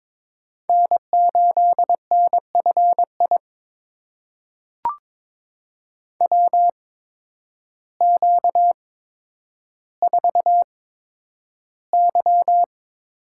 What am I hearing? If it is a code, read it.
Morse code, heard as a keyed tone, about 21 wpm, spaced as for 10 wpm, 700 Hz. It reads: N8NFI WQ4Y
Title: USA Call Signs - 8x Character Spacing - 22wpm